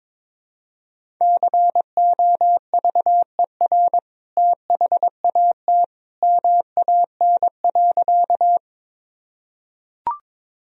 Morse code, heard as a keyed tone, about 22 wpm, 700 Hz. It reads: COVER THAT MAN.
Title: Sentences from Top 1000 Words - Review Part 1 - 22wpm